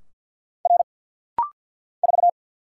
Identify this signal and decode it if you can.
Morse code, heard as a keyed tone, about 49 wpm, 700 Hz. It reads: R 4
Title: Single Letter-Number Alt - Rapid-Fire 50wpm